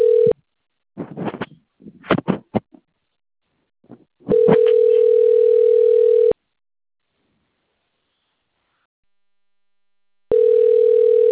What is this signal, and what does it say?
Ringback tone (North America), 440 + 480 Hz.
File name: rg-1600-unknown- - - .7206